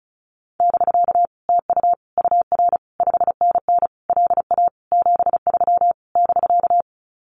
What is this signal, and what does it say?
Morse code, heard as a keyed tone, about 34 wpm, 700 Hz. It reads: <BK> TU UR 5NN LA 73 <BK>
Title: Common QSO Phrases Encore 35wpm